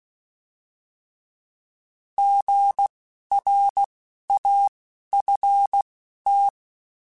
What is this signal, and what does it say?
Morse code, heard as a keyed tone, about 16 wpm, spaced as for 11 wpm, 800 Hz. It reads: GRAFT
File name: Alphabetical / Alphabetical mp3s / Gs @10